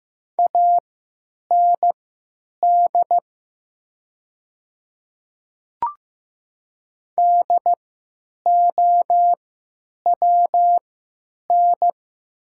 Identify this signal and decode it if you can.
Morse code, heard as a keyed tone, about 15 wpm, spaced as for 9 wpm, 700 Hz. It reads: AND DOWN
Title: 3x Character Spacing - Top 200 Words - 15wpm